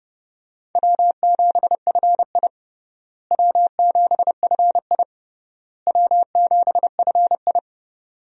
Morse code, heard as a keyed tone, about 30 wpm, 700 Hz. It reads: W7FI W7FI W7FI
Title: USA Call Signs - 4 Characters - Repeated 3x - 30wpm